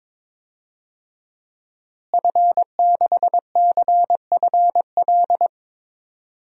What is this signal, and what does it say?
Morse code, heard as a keyed tone, about 22 wpm, 700 Hz. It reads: F6CFL
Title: DXCC-100 Call Signs - 5 Characters - Repeated 2x - 22wpm